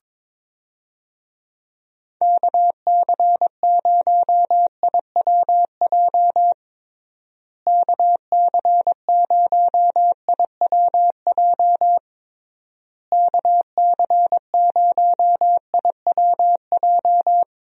Morse code, heard as a keyed tone, about 21 wpm, 700 Hz. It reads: KC0IWJ KC0IWJ KC0IWJ
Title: USA Call Signs - 6 Characters - Repeated 3x - 22wpm